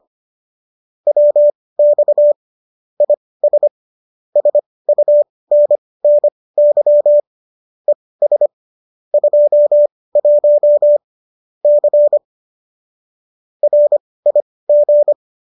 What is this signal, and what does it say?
Morse code, heard as a keyed tone, about 25 wpm, spaced as for 18 wpm, 590 Hz. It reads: WX IS SUNNY ES 21 C RIG
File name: qso215 18